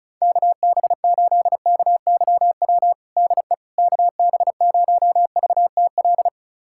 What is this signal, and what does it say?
Morse code, heard as a keyed tone, about 36 wpm, 700 Hz. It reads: KB8KYW DE KB0VTL
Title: USA Callsign DE Callsign 35wpm